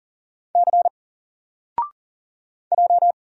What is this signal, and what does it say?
Morse code, heard as a keyed tone, about 39 wpm, 700 Hz. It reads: C J
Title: Single Character (Letter:Number:Punctuation:Prosign) 40wpm